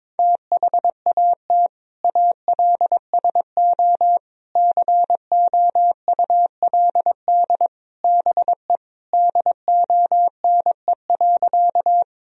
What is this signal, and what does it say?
Morse code, heard as a keyed tone, about 21 wpm, 700 Hz. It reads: THAT ALSO COULD BE DONE.
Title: Sentences from Top 1000 Words - Review Part 1 - 22wpm